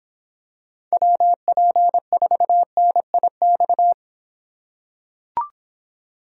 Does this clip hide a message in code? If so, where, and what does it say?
Yes: Morse code from 0.9 to 3.9 s, heard as a keyed tone, about 24 wpm, 700 Hz. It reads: WP4NIX